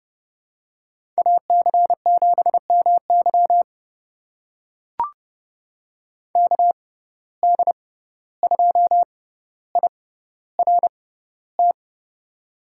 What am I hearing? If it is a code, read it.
Morse code, heard as a keyed tone, about 30 wpm, spaced as for 15 wpm, 700 Hz. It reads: AC7MY KD2IRT
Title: USA Call Signs - 6x Character Spacing - 30wpm